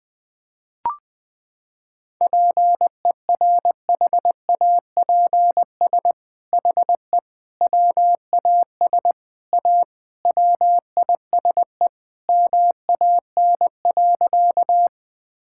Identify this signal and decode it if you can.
Morse code, heard as a keyed tone, about 21 wpm, 700 Hz. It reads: PERHAPS HE WAS A WISE MAN.
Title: Sentences from Top 1600 Words 20wpm